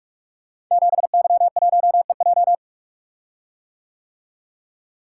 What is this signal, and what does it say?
Morse code, heard as a keyed tone, about 45 wpm, 700 Hz. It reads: 7Y1EJ
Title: Code Groups - 5 Letter:Numbers - 45wpm